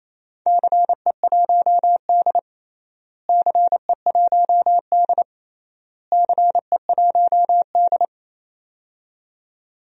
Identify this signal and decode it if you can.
Morse code, heard as a keyed tone, about 28 wpm, 700 Hz. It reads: CE1D CE1D CE1D
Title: DXCC-100 Call Signs - 4 Characters - Repeated 3x - 28wpm